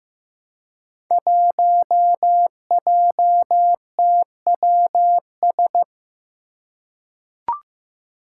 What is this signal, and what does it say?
Morse code, heard as a keyed tone, about 15 wpm, 700 Hz. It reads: 1JTWS
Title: Five Challenging Letter-Number Combinations 15wpm